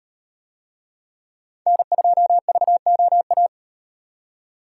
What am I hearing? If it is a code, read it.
Morse code, heard as a keyed tone, about 38 wpm, 700 Hz. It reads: N2VOA